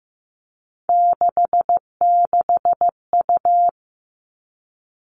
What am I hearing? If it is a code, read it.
Morse code, heard as a keyed tone, about 15 wpm, 700 Hz. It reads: 66U
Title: Three Challenging Letter-Number Combinations 15wpm